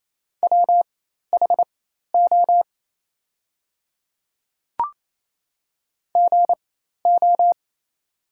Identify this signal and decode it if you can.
Morse code, heard as a keyed tone, about 28 wpm, spaced as for 13 wpm, 700 Hz. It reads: WHO GO
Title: 4x Character Spacing - Top 100 Words - 28wpm